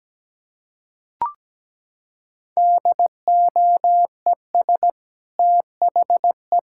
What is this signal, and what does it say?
Morse code, heard as a keyed tone, about 17 wpm, 700 Hz. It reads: DOES THE
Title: Sentences from Top 1600 Words 17wpm